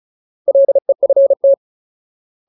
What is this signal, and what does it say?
Morse code, heard as a keyed tone, about 34 wpm, 550 Hz. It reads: LEFT
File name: Ultimate Word List / 35WPM Ultimate